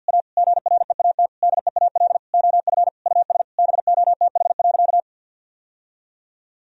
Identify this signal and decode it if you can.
Morse code, heard as a keyed tone, about 50 wpm, 700 Hz. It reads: A GREAT DEAL OF US BOTH.
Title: Sentences from Top 1000 Words - Review Part 4 - 50wpm